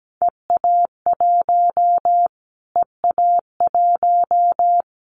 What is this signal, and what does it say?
Morse code, heard as a keyed tone, about 17 wpm, 700 Hz. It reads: EA1 EA1
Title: Callsign Prefixes for Top 25 DXCC - Repeated 3x - 17wpm